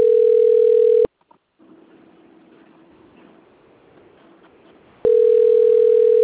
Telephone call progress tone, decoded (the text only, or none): ringback tone